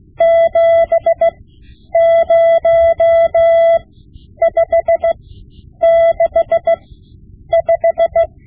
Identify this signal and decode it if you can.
Morse code, heard as a keyed tone, about 16 wpm, spaced as for 9 wpm, 670 Hz. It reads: 70565
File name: Morse Lessons / Morse Lessons / Tape-5B